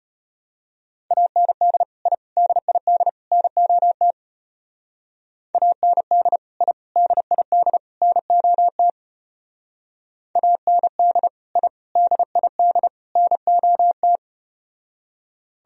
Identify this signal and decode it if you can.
Morse code, heard as a keyed tone, about 33 wpm, 700 Hz. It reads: AND I DID NOT AND I DID NOT AND I DID NOT